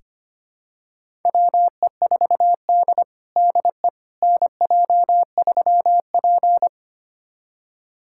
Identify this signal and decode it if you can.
Morse code, heard as a keyed tone, about 25 wpm, 700 Hz. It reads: WE4D DE NJ3P
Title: USA Callsign DE Callsign - 4 Characters - 25wpm